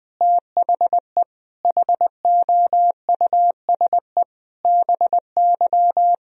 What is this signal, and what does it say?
Morse code, heard as a keyed tone, about 20 wpm, 700 Hz. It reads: THE HOUSE BY
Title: Sentences from Top 1000 Words - Review Part 4 - 20wpm